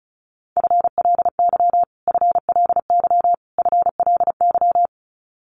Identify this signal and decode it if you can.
Morse code, heard as a keyed tone, about 34 wpm, 700 Hz. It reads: FLY FLY FLY
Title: Top 500 Words Repeated 35wpm